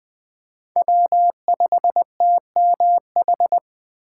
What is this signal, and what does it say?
Morse code, heard as a keyed tone, about 19 wpm, 700 Hz. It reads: W5TMH